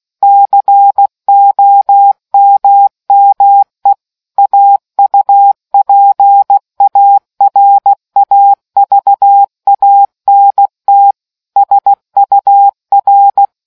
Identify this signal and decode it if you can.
Morse code, heard as a keyed tone, about 16 wpm, 800 Hz. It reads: COMME AUPARAVANT SUR